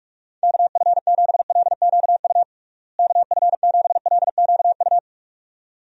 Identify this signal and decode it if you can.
Morse code, heard as a keyed tone, about 42 wpm, 700 Hz. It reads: KF7LQU KF7LQU
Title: USA Call Signs - 6 Characters - Repeated 2x - 45wpm